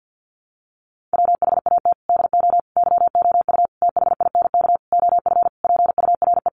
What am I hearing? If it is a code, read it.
Morse code, heard as a keyed tone, about 50 wpm, 700 Hz. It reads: WHAT DO YOU THINK OF PURE
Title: Sentences from Top 1500 Words 50wpm